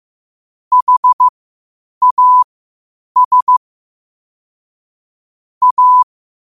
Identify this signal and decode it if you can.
Morse code, heard as a keyed tone, about 15 wpm, spaced as for 8 wpm, 1000 Hz. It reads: HAS A
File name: qotdfiles / 8wpm